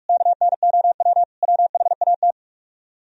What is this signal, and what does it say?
Morse code, heard as a keyed tone, about 44 wpm, 700 Hz. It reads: KNOW WHAT